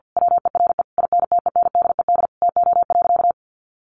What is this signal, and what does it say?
Morse code, heard as a keyed tone, about 50 wpm, 700 Hz. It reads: WERE INTENDED TO.
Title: A Sound Of Thunder 50wpm